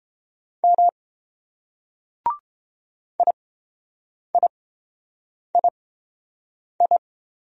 Morse code, heard as a keyed tone, about 30 wpm, spaced as for 7 wpm, 700 Hz. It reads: M IIII